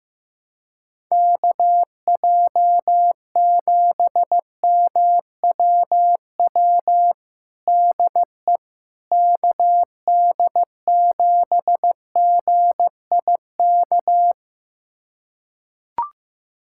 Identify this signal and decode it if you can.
Morse code, heard as a keyed tone, about 15 wpm, 700 Hz. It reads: KJ7MWW DE KD7GIK
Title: USA Callsign DE Callsign - 6 Characters - 15wpm